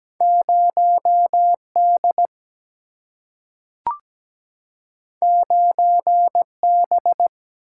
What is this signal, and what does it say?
Morse code, heard as a keyed tone, about 17 wpm, 700 Hz. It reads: 0 D 9 B
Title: Single-Letter Single-Number 17wpm